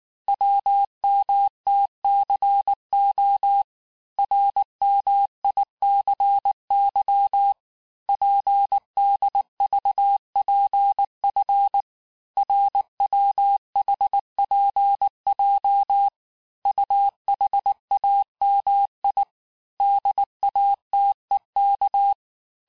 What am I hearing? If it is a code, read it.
Morse code, heard as a keyed tone, about 19 wpm, 800 Hz. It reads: WMTCO RMICY PDVPF RWHPJ UHAMI DATEK
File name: cours-du-lundi files / cours-1589-texte-chiffre-vitesse-18mpm